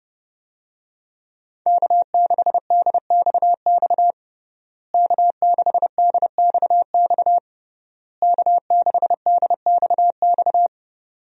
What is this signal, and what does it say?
Morse code, heard as a keyed tone, about 30 wpm, 700 Hz. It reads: K6DXX K6DXX K6DXX
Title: USA Call Signs - 5 Characters - Repeated 3x - 30wpm